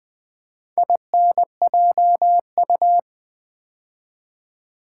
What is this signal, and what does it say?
Morse code, heard as a keyed tone, about 20 wpm, 700 Hz. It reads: INJU
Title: Build A Word - 2 Letters At A Time - 6-Letter Words - 20wpm